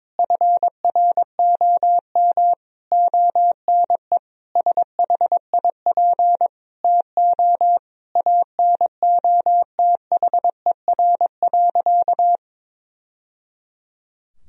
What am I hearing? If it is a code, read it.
Morse code, heard as a keyed tone, about 22 wpm, 700 Hz. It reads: FROM ONE SHIP TO ANOTHER.
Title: Sentences from Top 1000 Words - Review Part 2 - 22wpm